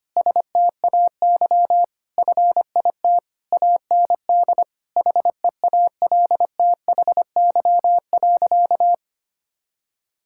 Morse code, heard as a keyed tone, about 25 wpm, 700 Hz. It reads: STAY FIT AND HEALTHY.
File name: practice / 25wpm / Difficult Sentences with Hints - 25wpm